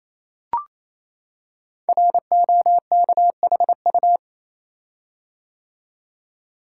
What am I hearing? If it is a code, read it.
Morse code, heard as a keyed tone, about 27 wpm, 700 Hz. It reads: ROKHU
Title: Code Groups - 5 Letters - 28wpm